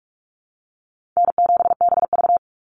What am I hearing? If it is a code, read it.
Morse code, heard as a keyed tone, about 44 wpm, 700 Hz. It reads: N7BV
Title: Easy Call Signs Repeated 3x - 45wpm